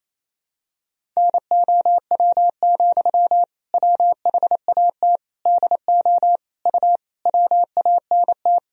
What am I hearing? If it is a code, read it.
Morse code, heard as a keyed tone, about 28 wpm, 700 Hz. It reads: NOW, WHAT DO U WANT
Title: Sentences from Top 200 Words and Abbreviations 28wpm